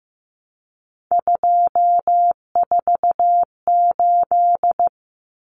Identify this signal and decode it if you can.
Morse code, heard as a keyed tone, about 15 wpm, 700 Hz. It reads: 248
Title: Numbers - 3-Digit 15wpm